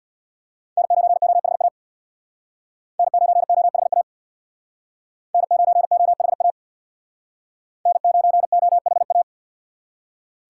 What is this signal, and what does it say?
Morse code, heard as a keyed tone, about 64 wpm, 700 Hz. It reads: N9OHA N9OHA N9OHA N9OHA